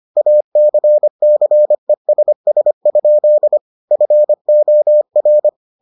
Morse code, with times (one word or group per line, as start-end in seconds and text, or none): 0.2-3.6 ACCESS?
3.9-5.5 FOR